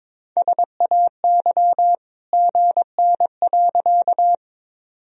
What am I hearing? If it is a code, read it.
Morse code, heard as a keyed tone, about 22 wpm, 700 Hz. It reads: SAY GN.
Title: Sentences from Top 300 Words and Abbreviations 22wpm